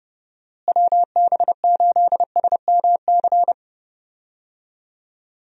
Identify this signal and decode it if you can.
Morse code, heard as a keyed tone, about 30 wpm, 700 Hz. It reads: WB8SMC
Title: USA Call Signs - 6 Characters - Repeated 2x - 30wpm